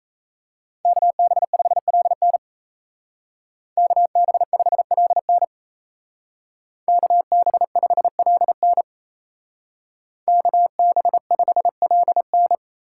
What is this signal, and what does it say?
Morse code, heard as a keyed tone, about 34 wpm, 700 Hz. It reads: KB5LN KB5LN KB5LN KB5LN
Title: Speed Racing - USA 2x2 Call Signs 28wpm